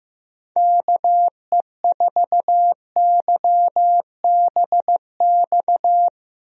Morse code, heard as a keyed tone, about 15 wpm, 700 Hz. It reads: KE4YBX
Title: USA Call Signs - 6 Characters - 15wpm